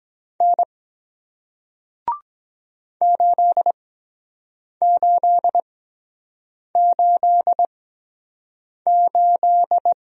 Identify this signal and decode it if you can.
Morse code, heard as a keyed tone, about 23 wpm, spaced as for 7 wpm, 700 Hz. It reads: N 8888